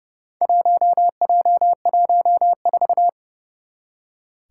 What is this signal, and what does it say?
Morse code, heard as a keyed tone, about 30 wpm, 700 Hz. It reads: 1J14
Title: Four Challenging Letter-Number Combinations 30wpm